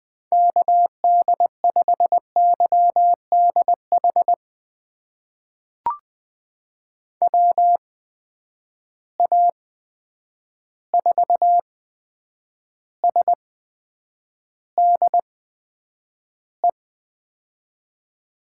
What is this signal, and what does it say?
Morse code, heard as a keyed tone, about 20 wpm, spaced as for 9 wpm, 700 Hz. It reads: KD5YDH WA4SDE
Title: USA Call Signs - 8x Character Spacing - 20wpm